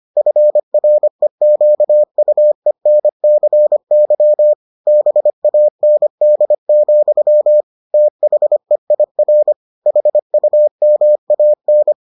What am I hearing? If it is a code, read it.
Morse code, heard as a keyed tone, about 25 wpm, 600 Hz. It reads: FREQUENCY BAND, THEIR HUMAN